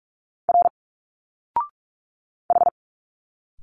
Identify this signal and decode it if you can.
Morse code, heard as a keyed tone, about 45 wpm, 700 Hz. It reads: R H
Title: Single Letters 45wpm